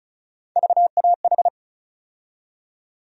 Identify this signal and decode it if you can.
Morse code, heard as a keyed tone, about 35 wpm, 700 Hz. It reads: VAH
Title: USA Call Sign Postfixes - 3 Letters 35wpm